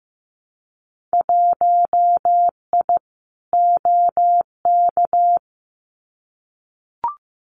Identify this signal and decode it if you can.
Morse code, heard as a keyed tone, about 15 wpm, 700 Hz. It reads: 1I OK
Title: Winter Field Day Exchanges Encore 15wpm